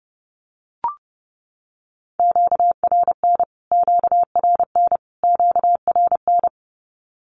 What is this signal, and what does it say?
Morse code, heard as a keyed tone, about 30 wpm, 700 Hz. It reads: QRN QRN QRN
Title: Q-Codes Repeated 3x 30wpm